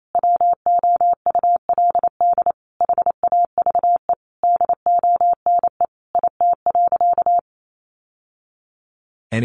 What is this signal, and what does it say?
Morse code, heard as a keyed tone, about 28 wpm, 700 Hz. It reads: WOULD HAVE DONE IT.
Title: Sentences from Top 1200 Words 28wpm